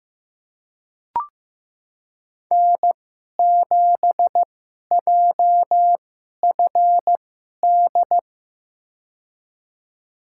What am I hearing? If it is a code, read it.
Morse code, heard as a keyed tone, about 15 wpm, spaced as for 11 wpm, 700 Hz. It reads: N7JFD
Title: USA Call Signs - 5 Characters - 2x Character Spacing - 15wpm